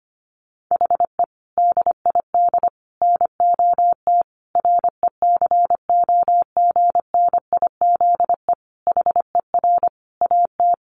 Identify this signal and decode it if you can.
Morse code, heard as a keyed tone, about 25 wpm, 700 Hz. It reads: HE DID NOT RECOGNIZE HER AT
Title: Sentences from Top 1400 Words 25wpm